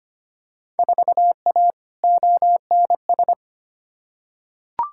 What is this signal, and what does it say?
Morse code, heard as a keyed tone, about 25 wpm, 700 Hz. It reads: 4A ONS
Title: Field Day Exchanges - 25wpm